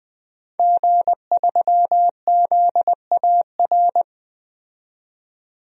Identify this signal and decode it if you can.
Morse code, heard as a keyed tone, about 21 wpm, 700 Hz. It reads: G3ZAR
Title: DXCC-100 Call Signs - 5 Characters - 20wpm